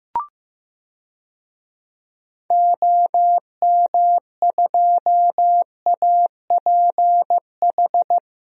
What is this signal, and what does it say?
Morse code, heard as a keyed tone, about 15 wpm, 700 Hz. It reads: OM2APH
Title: DXCC-100 Call Signs - 6 Characters - 15wpm